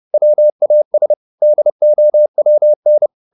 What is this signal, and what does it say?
Morse code, heard as a keyed tone, about 30 wpm, 600 Hz. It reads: WAS DOWN